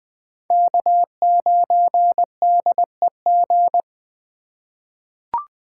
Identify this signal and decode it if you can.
Morse code, heard as a keyed tone, about 20 wpm, 700 Hz. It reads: K9DEG
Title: 2023 POTA Hunter Call Signs - 20wpm